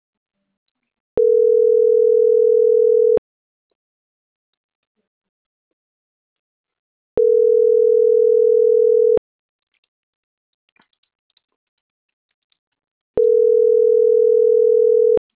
Ringback tone (North America), 440 + 480 Hz.